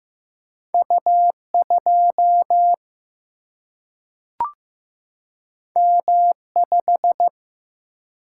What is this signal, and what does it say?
Morse code, heard as a keyed tone, about 15 wpm, 700 Hz. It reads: U2 M5
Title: Single-Letter Single-Number 15wpm